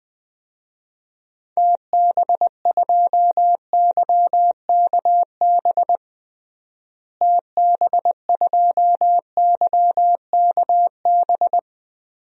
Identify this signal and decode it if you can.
Morse code, heard as a keyed tone, about 20 wpm, 700 Hz. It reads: TB2YKB TB2YKB
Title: DXCC-100 Call Signs - 6 Characters - Repeated 2x - 20wpm